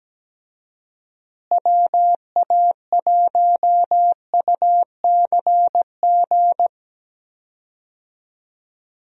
Morse code, heard as a keyed tone, about 17 wpm, 700 Hz. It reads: WA1UCG
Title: USA Call Signs - 6 Characters - Repeated 2x - 17wpm